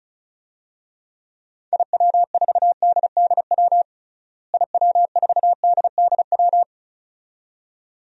Morse code, heard as a keyed tone, about 34 wpm, 700 Hz. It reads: IW4DDW IW4DDW
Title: DXCC-100 Call Signs - 6 Characters - Repeated 2x - 35wpm